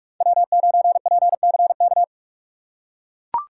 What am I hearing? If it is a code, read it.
Morse code, heard as a keyed tone, about 42 wpm, 700 Hz. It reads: W9PCK